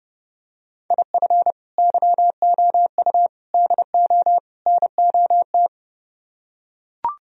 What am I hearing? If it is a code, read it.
Morse code, heard as a keyed tone, about 30 wpm, 700 Hz. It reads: IF YOU DO NOT